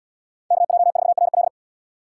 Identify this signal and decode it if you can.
Morse code, heard as a keyed tone, about 77 wpm, 700 Hz. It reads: KJ4RL